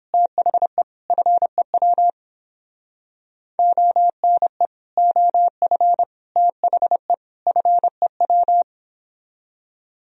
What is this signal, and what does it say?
Morse code, heard as a keyed tone, about 31 wpm, 700 Hz. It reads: THE FEW ONE OF THE FEW